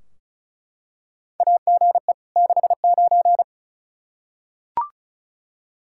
Morse code, heard as a keyed tone, about 36 wpm, 700 Hz. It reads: AGE 69
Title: Practice QSO - Exchange 3 - More Info - 35wpm